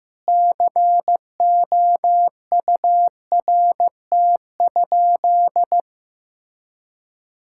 Morse code, heard as a keyed tone, about 16 wpm, 700 Hz. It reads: COURT?
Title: Sentences from Top 1600 Words 15wpm